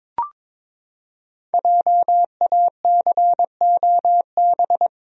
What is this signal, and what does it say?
Morse code, heard as a keyed tone, about 22 wpm, 700 Hz. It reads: JACOB
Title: USA Common Names - 22wpm